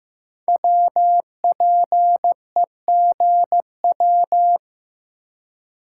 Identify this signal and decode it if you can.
Morse code, heard as a keyed tone, about 15 wpm, 700 Hz. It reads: WPEGW
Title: Code Groups - 5 Letter:Numbers - 15wpm